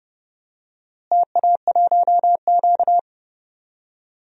Morse code, heard as a keyed tone, about 30 wpm, 700 Hz. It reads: TA1Q
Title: DXCC-100 Call Signs - 4 Characters - Repeated 2x - 30wpm